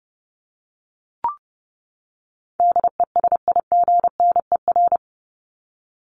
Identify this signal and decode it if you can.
Morse code, heard as a keyed tone, about 30 wpm, 700 Hz. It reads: DESIGNER
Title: Top 1501-1750 Words 30wpm